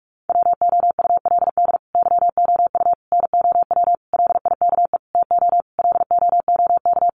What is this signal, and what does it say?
Morse code, heard as a keyed tone, about 44 wpm, 700 Hz. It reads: WOULD YOU NOW LIKE TO LOOK
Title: Sentences from Top 200 Words 45wpm